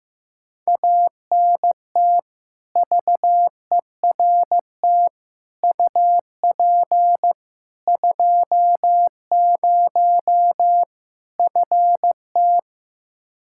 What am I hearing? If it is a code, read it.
Morse code, heard as a keyed tone, about 15 wpm, 700 Hz. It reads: ANT VERT UP 20 FT
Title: K1USN SST QSO Practice 15wpm